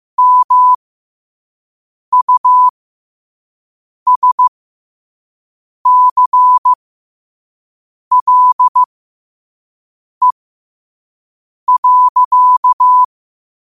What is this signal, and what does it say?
Morse code, heard as a keyed tone, about 15 wpm, spaced as for 5 wpm, 1000 Hz. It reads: MUSCLE.
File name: qotdfiles / 5wpm